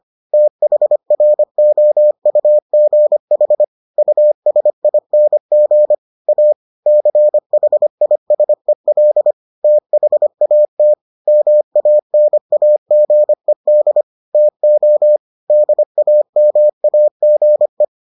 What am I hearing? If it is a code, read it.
Morse code, heard as a keyed tone, about 25 wpm, 600 Hz. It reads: THROUGH USING A CHISEL THAT MANAGED TO DAMAGE